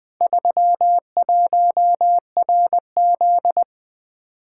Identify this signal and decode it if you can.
Morse code, heard as a keyed tone, about 20 wpm, 700 Hz. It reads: 31RZ
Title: Four Letter-Number Combinations - 20wpm